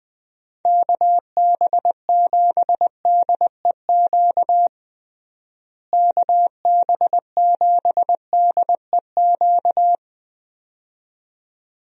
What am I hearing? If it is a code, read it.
Morse code, heard as a keyed tone, about 21 wpm, 700 Hz. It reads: KB7DEQ KB7DEQ
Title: USA Call Signs - 6 Characters - Repeated 2x - 20wpm